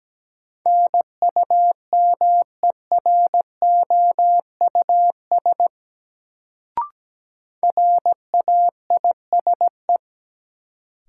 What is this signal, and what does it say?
Morse code, heard as a keyed tone, about 17 wpm, 700 Hz. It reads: NUMEROUS RAISE
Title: Sentences from Top 1400 Words - New Word Drill Alt - 17wpm